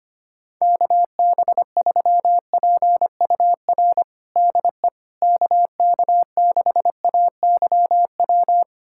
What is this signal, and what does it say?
Morse code, heard as a keyed tone, about 25 wpm, 700 Hz. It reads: KB3PUR DE KK6AYW
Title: USA Callsign DE Callsign - 6 Characters - 25wpm